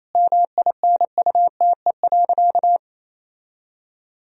Morse code, heard as a keyed tone, about 28 wpm, 700 Hz. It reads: MINUTE.
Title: Sentences from Top 1000 Words - Review Part 2 - 28wpm